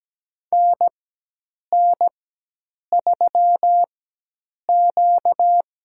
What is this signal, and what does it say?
Morse code, heard as a keyed tone, about 17 wpm, spaced as for 8 wpm, 700 Hz. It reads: NN3Q
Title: USA Call Signs - 4 Characters - 4x Character Spacing - 17wpm